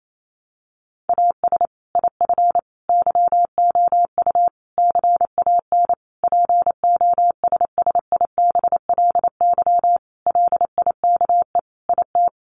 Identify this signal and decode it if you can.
Morse code, heard as a keyed tone, about 28 wpm, 700 Hz. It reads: AS IF YOU CAN POSSIBLY LIKE IT
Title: Sentences from Top 1300 Words 28wpm